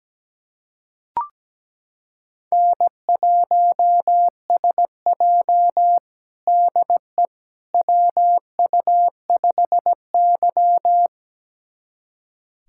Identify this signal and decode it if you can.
Morse code, heard as a keyed tone, about 17 wpm, 700 Hz. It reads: N1SJ DE WU5Y
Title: USA Callsign DE Callsign - 4 Characters - 17wpm